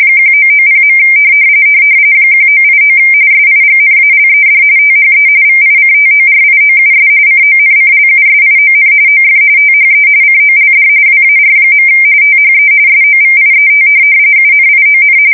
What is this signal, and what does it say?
RTTY, about 45.45 baud. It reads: 1990. 1989 SIGNALED THE VICTORIOUS CONCLUSION OF THE COLD WAR. THE SOVIET UNION
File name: fsk170 / USS-New-Jersey